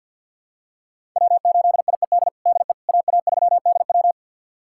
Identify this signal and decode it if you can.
Morse code, heard as a keyed tone, about 51 wpm, 700 Hz. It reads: W8IED DE AA3DW
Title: USA Callsign DE Callsign - 5 Characters - 50wpm